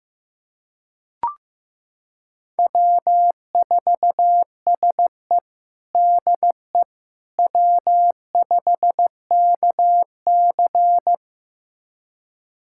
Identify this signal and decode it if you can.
Morse code, heard as a keyed tone, about 15 wpm, 700 Hz. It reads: W4SE DE W5KC
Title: USA Callsign DE Callsign - 4 Characters - 15wpm